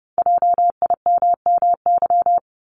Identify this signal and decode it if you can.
Morse code, heard as a keyed tone, about 30 wpm, 700 Hz. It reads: JIMMY
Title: Multi-letter Phonemes 30wpm